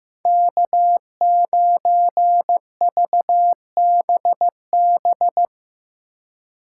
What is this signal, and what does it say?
Morse code, heard as a keyed tone, about 17 wpm, 700 Hz. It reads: K9VBB
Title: Speed Racing - USA 1x3 Call Signs 15wpm